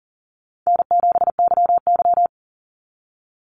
Morse code, heard as a keyed tone, about 40 wpm, 700 Hz. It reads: N7YY
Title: NA SOTA Chaser Callsigns 40wpm